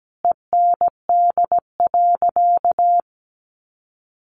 Morse code, heard as a keyed tone, about 17 wpm, 700 Hz. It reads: END.
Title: Easy Phrases 17wpm